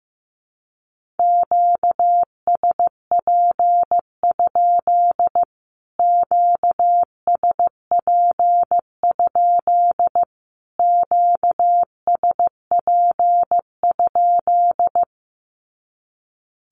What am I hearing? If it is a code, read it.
Morse code, heard as a keyed tone, about 15 wpm, 700 Hz. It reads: QSP? QSP? QSP?
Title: Q-Codes Repeated 3x 15wpm